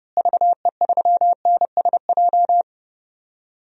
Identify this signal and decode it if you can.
Morse code, heard as a keyed tone, about 31 wpm, 700 Hz. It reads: VE3NSJ
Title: DXCC-100 Call Signs - 6 Characters - Repeated 3x - 30wpm